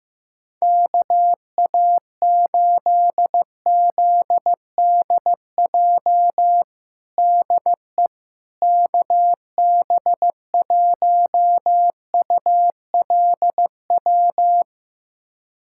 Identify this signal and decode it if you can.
Morse code, heard as a keyed tone, about 15 wpm, 700 Hz. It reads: KA8ZDJ DE KB1ULW
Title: USA Callsign DE Callsign 15wpm